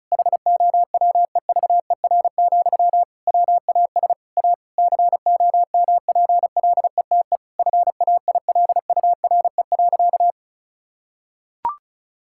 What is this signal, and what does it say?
Morse code, heard as a keyed tone, about 36 wpm, 700 Hz. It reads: HOWEVER, WAS A COMPLETE FAILURE.
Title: Sentences from Top 1000 Words - Review Part 6 - 35wpm